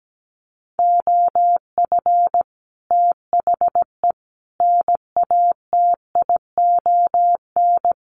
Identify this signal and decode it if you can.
Morse code, heard as a keyed tone, about 17 wpm, 700 Hz. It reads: OF THE NATION
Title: Sets of 3 Words 17wpm